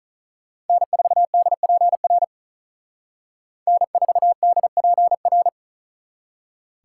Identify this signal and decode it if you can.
Morse code, heard as a keyed tone, about 43 wpm, 700 Hz. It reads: N4DPR N4DPR
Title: Speed Racing - USA 1x3 Call Signs 35wpm